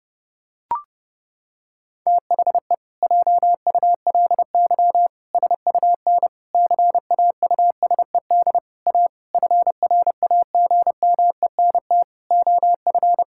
Morse code, heard as a keyed tone, about 30 wpm, 700 Hz. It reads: THE JULY SUN CAUSED A FRAGMENT OF
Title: Pangram Challenge v2 - 30wpm